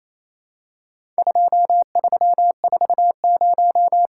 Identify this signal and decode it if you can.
Morse code, heard as a keyed tone, about 28 wpm, 700 Hz. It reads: 2340
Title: Worked All Europe DX Contest - QTC Exchange - 28wpm